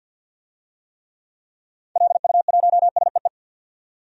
Morse code, heard as a keyed tone, about 51 wpm, 700 Hz. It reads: LU1SEE
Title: DXCC-100 Call Signs - 6 Characters - 50wpm